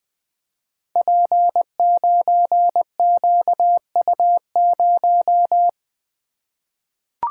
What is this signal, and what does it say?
Morse code, heard as a keyed tone, about 20 wpm, 700 Hz. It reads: P9QU0
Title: Code Groups - 5 Letter:Numbers - 20wpm